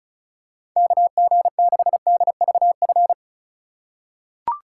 Morse code, heard as a keyed tone, about 36 wpm, 700 Hz. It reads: KG6DVF
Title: USA Call Signs - 6 Characters - 3x Character Spacing - 35wpm